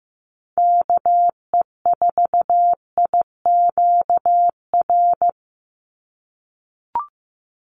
Morse code, heard as a keyed tone, about 15 wpm, 700 Hz. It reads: KE4IQR
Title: USA Call Signs 15wpm